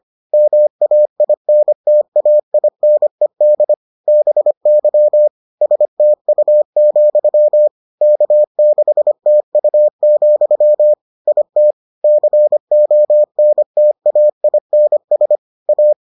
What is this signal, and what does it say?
Morse code, heard as a keyed tone, about 25 wpm, 600 Hz. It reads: MAINTAINED BY STU, K6TU, IT CONTAINS A